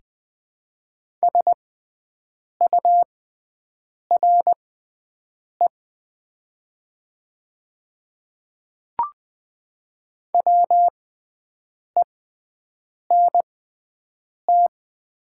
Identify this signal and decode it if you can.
Morse code, heard as a keyed tone, about 20 wpm, spaced as for 7 wpm, 700 Hz. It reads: SURE WENT
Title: 6x Character Spacing - Top 300 Words - 20wpm